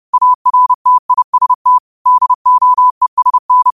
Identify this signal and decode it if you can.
Morse code, heard as a keyed tone, about 30 wpm, 1000 Hz. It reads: ARTIST DOESN
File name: qotdfiles / 30wpm